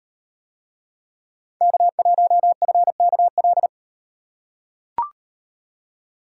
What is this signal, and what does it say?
Morse code, heard as a keyed tone, about 36 wpm, 700 Hz. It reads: K1FKL